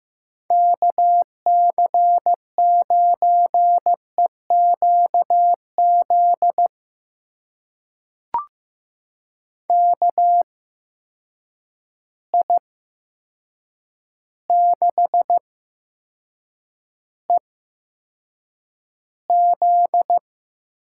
Morse code, heard as a keyed tone, about 15 wpm, spaced as for 7 wpm, 700 Hz. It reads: KC9EQZ KI6EZ
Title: USA Call Signs - 6 Characters - 8x Character Spacing - 15wpm